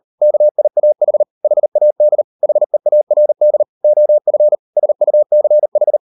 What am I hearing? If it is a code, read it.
Morse code, heard as a keyed tone, about 38 wpm, 600 Hz. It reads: KIAH HAD HEARD OF SUCH